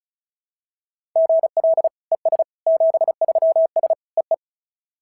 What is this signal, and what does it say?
Morse code, heard as a keyed tone, about 36 wpm, 690 Hz. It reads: GL ES 73S EE
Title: SOTA QSO Practice 35wpm